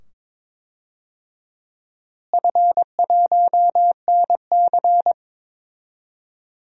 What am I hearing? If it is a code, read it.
Morse code, heard as a keyed tone, about 22 wpm, 700 Hz. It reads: F1NC
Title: DXCC-100 Call Signs - 4 Characters - Repeated 2x - 22wpm